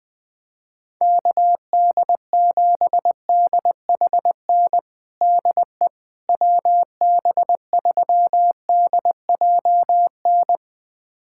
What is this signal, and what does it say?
Morse code, heard as a keyed tone, about 20 wpm, 700 Hz. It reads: KD7DHN DE WB3DJN
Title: USA Callsign DE Callsign 20wpm